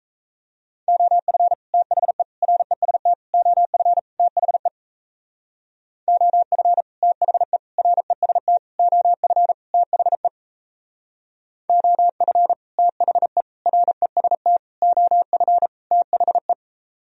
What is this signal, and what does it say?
Morse code, heard as a keyed tone, about 34 wpm, 700 Hz. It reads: OF THE REST OF THE OF THE REST OF THE OF THE REST OF THE
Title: Speed-Racing - Sets of 5 Words 28wpm